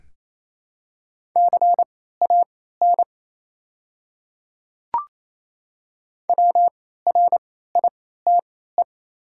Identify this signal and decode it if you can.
Morse code, heard as a keyed tone, about 28 wpm, spaced as for 16 wpm, 700 Hz. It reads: CAN WRITE
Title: 3x Character Spacing - Top 100 Words - 28wpm